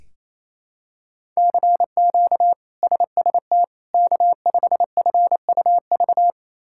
Morse code, heard as a keyed tone, about 28 wpm, 700 Hz. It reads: CQ SST K5FUV
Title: K1USN SST QSO Practice - 28wpm